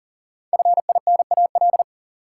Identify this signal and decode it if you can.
Morse code, heard as a keyed tone, about 40 wpm, 700 Hz. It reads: FINAL
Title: Sentences from Top 800 Words - New Word Drill - 40wpm